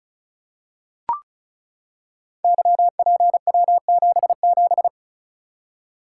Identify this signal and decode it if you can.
Morse code, heard as a keyed tone, about 36 wpm, 700 Hz. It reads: YPW77
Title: Code Groups - 5 Letter:Numbers - 35wpm